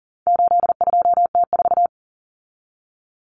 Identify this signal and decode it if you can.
Morse code, heard as a keyed tone, about 40 wpm, 700 Hz. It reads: 82T4
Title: Common Cut Numbers - 4-Digit 40wpm